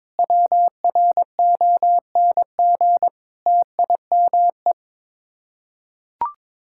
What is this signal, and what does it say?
Morse code, heard as a keyed tone, about 22 wpm, 700 Hz. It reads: WRONG TIME
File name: practice / 22wpm / Sets of 8 Words - 22wpm